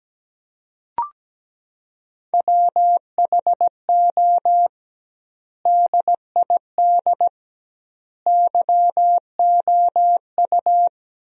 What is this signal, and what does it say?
Morse code, heard as a keyed tone, about 17 wpm, 700 Hz. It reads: WHO DID YOU
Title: 2x Word-Spacing - Sentences from Top 300 Words - 17wpm